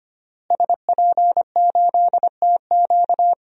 Morse code, heard as a keyed tone, about 25 wpm, 700 Hz. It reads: SP8TQ
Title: DXCC-100 Call Signs - 5 Characters - Repeated 2x - 25wpm